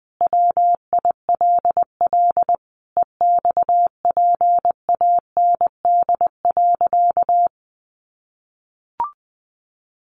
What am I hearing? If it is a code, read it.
Morse code, heard as a keyed tone, about 20 wpm, 700 Hz. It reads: WILL EXPAND.
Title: Sentences from Top 1400 Words 20wpm